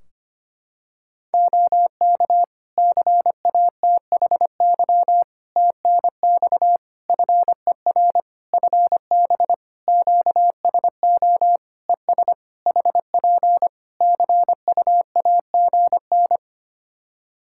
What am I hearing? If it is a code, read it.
Morse code, heard as a keyed tone, about 25 wpm, 700 Hz. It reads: OK CATHY TNX FER FB QSO ES HP CUAGN
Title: Practice QSO - Exchange 4 - Ending - 25wpm